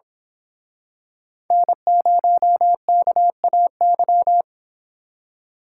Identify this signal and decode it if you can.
Morse code, heard as a keyed tone, about 26 wpm, 700 Hz. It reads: N0KAY
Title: Speed Racing - USA 1x3 Call Signs 22wpm